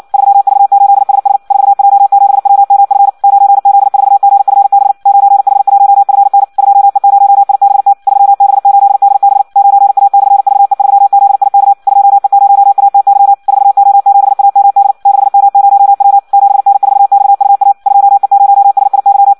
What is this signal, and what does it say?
Morse code, heard as a keyed tone, about 56 wpm, 800 Hz. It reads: 3V8SS 4J7WMF 9Z4GFB 9H1FN 2E0IBT 3B8DB 9A/VE3ZIK 2E0ATY 4O7AMD 6M0W 7N4XUA 2E0SIY
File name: Audio practica CW / qrq-tx-58 2